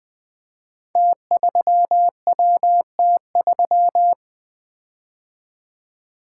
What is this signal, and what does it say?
Morse code, heard as a keyed tone, about 21 wpm, 700 Hz. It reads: T3WT3
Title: Code Groups - 5 Letter:Numbers - 20wpm